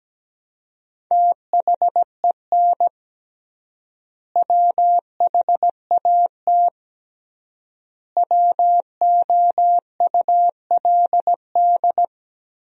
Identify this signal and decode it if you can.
Morse code, heard as a keyed tone, about 17 wpm, 700 Hz. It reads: THEN WHAT WOULD
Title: 3x Word-Spacing - Sentences from Top 400 Words - 17wpm